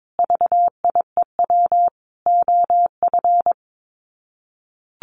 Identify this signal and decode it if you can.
Morse code, heard as a keyed tone, about 22 wpm, 700 Hz. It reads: VIEW OF
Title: Sets of 5 Words 22wpm